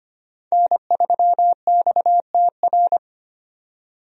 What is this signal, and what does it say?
Morse code, heard as a keyed tone, about 25 wpm, 700 Hz. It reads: N3XTR
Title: USA Call Signs - 5 Characters - 25wpm